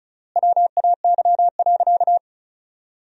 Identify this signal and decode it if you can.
Morse code, heard as a keyed tone, about 36 wpm, 700 Hz. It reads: WAY.
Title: 3x Word-Spacing - Sentences from Top 300 Words - 35wpm